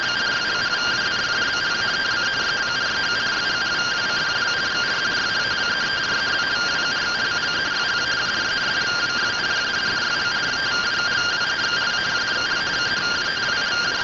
Two-tone teleprinter signal, about 170 Hz shift.